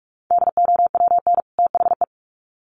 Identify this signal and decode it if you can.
Morse code, heard as a keyed tone, about 44 wpm, 700 Hz. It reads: DOWN THE
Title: Sets of 2 Words 45wpm